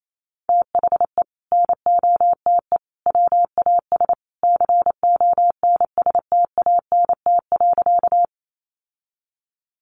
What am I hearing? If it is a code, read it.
Morse code, heard as a keyed tone, about 28 wpm, 700 Hz. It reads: THE NOTE WAS CONSTANT.
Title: Sentences from Top 1500 Words 28wpm